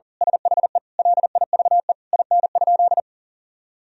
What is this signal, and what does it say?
Morse code, heard as a keyed tone, about 40 wpm, 700 Hz. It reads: SHE LIVE IN?
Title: Sentences from Top 1000 Words - Review Part 2 - 40wpm